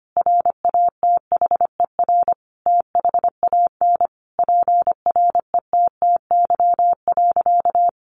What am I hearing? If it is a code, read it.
Morse code, heard as a keyed tone, about 25 wpm, 700 Hz. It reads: RATHER THAN PRETTY.
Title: Sentences from Top 1000 Words 25wpm